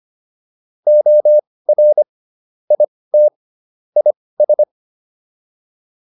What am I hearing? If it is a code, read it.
Morse code, heard as a keyed tone, about 25 wpm, spaced as for 18 wpm, 600 Hz. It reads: OR IT IS